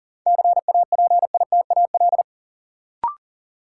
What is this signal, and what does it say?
Morse code, heard as a keyed tone, about 39 wpm, 700 Hz. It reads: CAPITAL
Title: Sentences from Top 1000 Words - New Word Drill - 40wpm